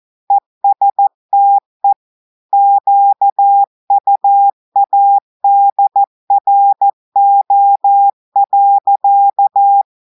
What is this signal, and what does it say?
Morse code, heard as a keyed tone, about 14 wpm, 800 Hz. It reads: ESTE QUADRO.
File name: audios / portugais / portugais texte en clair 1 vitesse 14 mots minutes